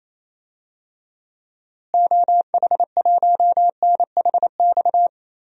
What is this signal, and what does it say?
Morse code, heard as a keyed tone, about 28 wpm, 700 Hz. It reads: OH1NHX
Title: DXCC-100 Call Signs - 6 Characters - Repeated 3x - 28wpm